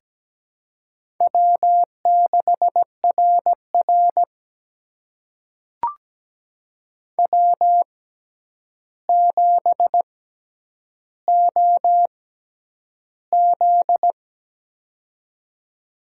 Morse code, heard as a keyed tone, about 17 wpm, spaced as for 9 wpm, 700 Hz. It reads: W6RR W7OZ
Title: USA Call Signs - 4 Characters - 6x Character Spacing - 17wpm